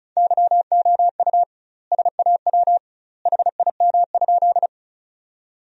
Morse code, heard as a keyed tone, about 34 wpm, 700 Hz. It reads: YOU SAW HIM?
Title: 2x Word-Spacing - Sentences from Top 300 Words - 35wpm